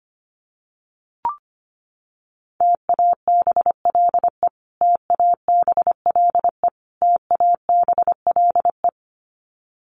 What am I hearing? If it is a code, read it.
Morse code, heard as a keyed tone, about 25 wpm, 700 Hz. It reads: TABLE TABLE TABLE
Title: Top 500 Words Repeated 25wpm